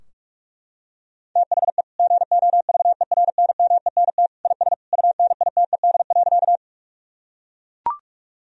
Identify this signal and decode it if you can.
Morse code, heard as a keyed tone, about 41 wpm, 700 Hz. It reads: THE GOVERNMENT IS UNITED.
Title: Sentences from Top 1000 Words - Review Part 4 - 45wpm